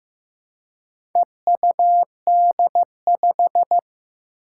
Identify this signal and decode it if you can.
Morse code, heard as a keyed tone, about 15 wpm, 700 Hz. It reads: EUD5
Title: Four Challenging Letter-Number Combinations 15wpm